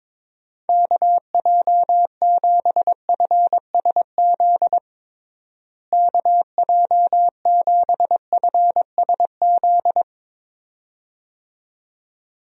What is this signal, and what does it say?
Morse code, heard as a keyed tone, about 22 wpm, 700 Hz. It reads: KJ7FSZ KJ7FSZ
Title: USA Call Signs - 6 Characters - Repeated 2x - 22wpm